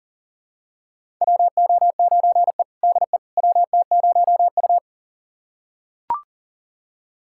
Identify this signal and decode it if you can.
Morse code, heard as a keyed tone, about 40 wpm, 700 Hz. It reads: WO9E DE WT0U
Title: USA Callsign DE Callsign - 4 Characters - 40wpm